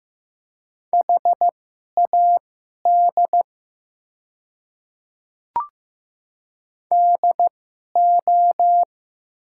Morse code, heard as a keyed tone, about 15 wpm, spaced as for 11 wpm, 700 Hz. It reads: HAD DO
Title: 2x Character Spacing - Top 100 Words - 15wpm